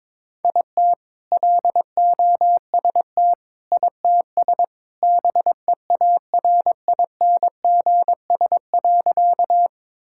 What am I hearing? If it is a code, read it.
Morse code, heard as a keyed tone, about 21 wpm, 700 Hz. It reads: IT LOST ITS BEARINGS.
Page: morse-code-ninja-practice-22wpm